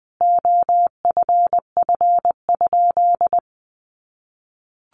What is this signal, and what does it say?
Morse code, heard as a keyed tone, about 20 wpm, 700 Hz. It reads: OFF?
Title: Sentences from Top 500 Words 20wpm